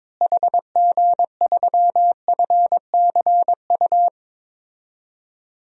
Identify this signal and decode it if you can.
Morse code, heard as a keyed tone, about 22 wpm, 700 Hz. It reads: HG3FCU
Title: DXCC-100 Call Signs - 6 Characters - Repeated 3x - 22wpm